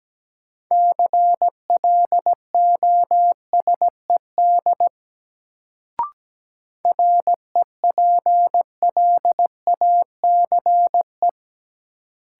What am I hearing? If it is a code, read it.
Morse code, heard as a keyed tone, about 17 wpm, 700 Hz. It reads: CLOSED REPLACE
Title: Sentences from Top 1400 Words - New Word Drill Alt - 17wpm